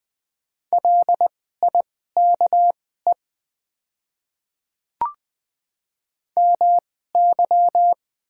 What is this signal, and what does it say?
Morse code, heard as a keyed tone, about 20 wpm, spaced as for 14 wpm, 700 Hz. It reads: LIKE MY
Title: 2x Character Spacing - Top 100 Words - 20wpm